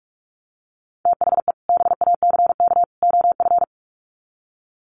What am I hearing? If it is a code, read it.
Morse code, heard as a keyed tone, about 44 wpm, 700 Hz. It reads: THE BACK OF
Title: Sets of 3 Words 45wpm